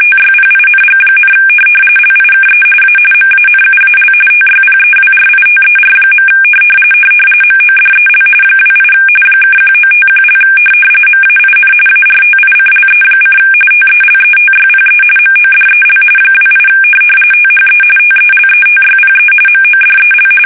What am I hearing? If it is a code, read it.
RTTY, about 45.45 baud. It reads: TYPING, LIKE A MANUAL TYPEWRITER. MODEL 28 (1951) MAIN ARTICLE: TELETYPE MODEL 28 THE TELETYPE MODEL
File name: fsk850 / teletype-corp